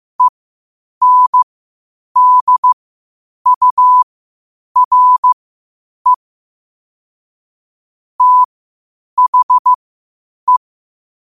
Morse code, heard as a keyed tone, about 15 wpm, spaced as for 8 wpm, 1000 Hz. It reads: ENDURE THE
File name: qotdfiles / 8wpm